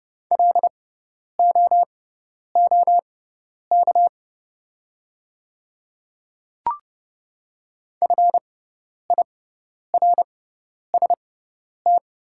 Morse code, heard as a keyed tone, about 30 wpm, spaced as for 10 wpm, 690 Hz. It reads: LOOK FIRST
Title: 6x Character Spacing - Top 100 Words - 30wpm